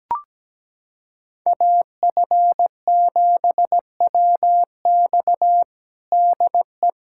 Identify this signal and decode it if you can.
Morse code, heard as a keyed tone, about 17 wpm, 700 Hz. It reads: AF7WX DE
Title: USA Callsign DE Callsign - 5 Characters - 17wpm